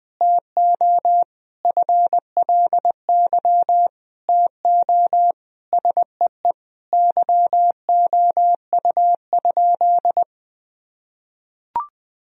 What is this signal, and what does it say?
Morse code, heard as a keyed tone, about 20 wpm, 700 Hz. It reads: TO FLY TO SEE YOU?
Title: Sentences from Top 1000 Words - Review Part 2 - 20wpm